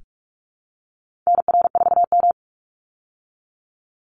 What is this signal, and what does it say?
Morse code, heard as a keyed tone, about 44 wpm, 700 Hz. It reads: NR4M
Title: Call Signs Repeated 2x - 45wpm